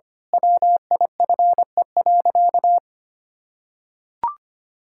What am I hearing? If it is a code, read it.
Morse code, heard as a keyed tone, about 25 wpm, 700 Hz. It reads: WIFE.
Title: Pangram Challenge v2 - 25wpm